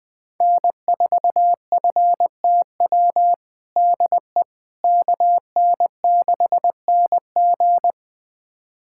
Morse code, heard as a keyed tone, about 20 wpm, 700 Hz. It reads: N4FTW DE KN6NG
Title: USA Callsign DE Callsign - 5 Characters - 20wpm